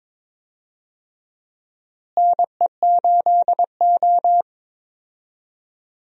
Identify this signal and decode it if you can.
Morse code, heard as a keyed tone, about 22 wpm, 700 Hz. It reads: NE8O
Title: USA Call Signs - 4 Characters - Repeated 2x - 22wpm